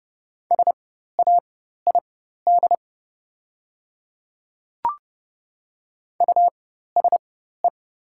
Morse code, heard as a keyed tone, about 30 wpm, spaced as for 14 wpm, 700 Hz. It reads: SAID USE
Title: 4x Character Spacing - Top 100 Words - 30wpm